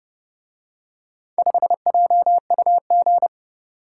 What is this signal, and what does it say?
Morse code, heard as a keyed tone, about 29 wpm, 700 Hz. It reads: 5JUG